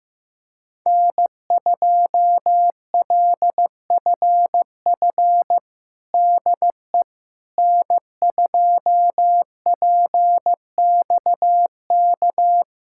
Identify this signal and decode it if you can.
Morse code, heard as a keyed tone, about 15 wpm, 700 Hz. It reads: N2LFF DE N2PXK
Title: USA Callsign DE Callsign - 5 Characters - 15wpm